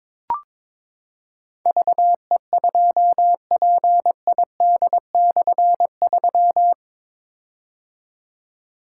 Morse code, heard as a keyed tone, about 22 wpm, 700 Hz. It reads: VE2PID/3
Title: K1USN SST QSO Practice - 22wpm